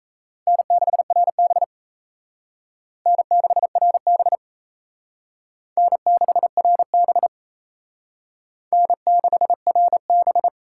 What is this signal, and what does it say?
Morse code, heard as a keyed tone, about 35 wpm, 700 Hz. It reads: N6RB N6RB N6RB N6RB
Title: Speed Racing - USA Call Signs 28wpm